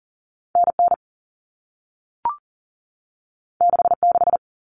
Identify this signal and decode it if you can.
Morse code, heard as a keyed tone, about 40 wpm, 700 Hz. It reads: NN 66
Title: Common Cut Numbers - 2-Digit 40wpm